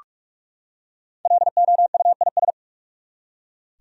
Morse code, heard as a keyed tone, about 46 wpm, 700 Hz. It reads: LOUIS
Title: USA Common Names - Max 5 Letters - 45wpm